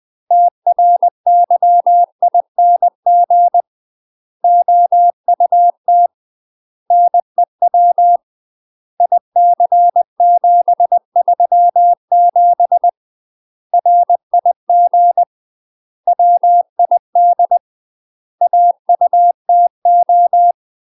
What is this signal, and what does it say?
Morse code, heard as a keyed tone, about 20 wpm, 700 Hz. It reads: TRYING OUT NEW IC737 RIG WID AUTO
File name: UK MASTER QSOs TRAINING / 20WPM / UK MASTER QSOs 0047